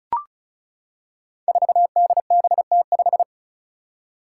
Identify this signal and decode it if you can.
Morse code, heard as a keyed tone, about 36 wpm, 700 Hz. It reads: 4DBT5
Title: Five Challenging Letter-Number Combinations 35wpm